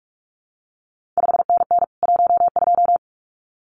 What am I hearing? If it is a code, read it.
Morse code, heard as a keyed tone, about 46 wpm, 700 Hz. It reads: 5NN 12
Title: CQ WW DX Contest Exchange Encore - 45wpm